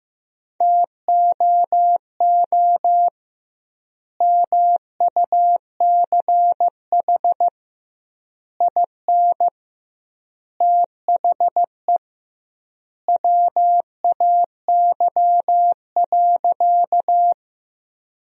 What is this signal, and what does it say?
Morse code, heard as a keyed tone, about 16 wpm, 700 Hz. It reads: TOO MUCH IN THE WAY.
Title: 2x Word-Spacing - Sentences from Top 200 Words - 15wpm